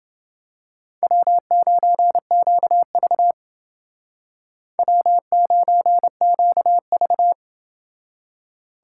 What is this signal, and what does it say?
Morse code, heard as a keyed tone, about 28 wpm, 700 Hz. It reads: W9QV W9QV